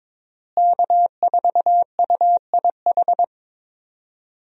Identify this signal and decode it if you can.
Morse code, heard as a keyed tone, about 22 wpm, 700 Hz. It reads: K4UIH
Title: USA Call Signs - 5 Characters - Repeated 2x - 22wpm